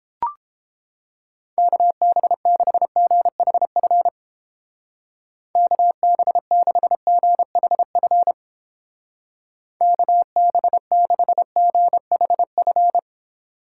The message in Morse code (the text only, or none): KB6GHF KB6GHF KB6GHF